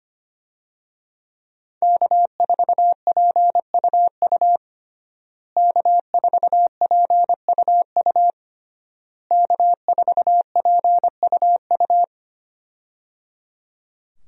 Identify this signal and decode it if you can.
Morse code, heard as a keyed tone, about 25 wpm, 700 Hz. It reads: K4PUU K4PUU K4PUU
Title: USA Call Signs - 5 Characters - Repeated 3x - 25wpm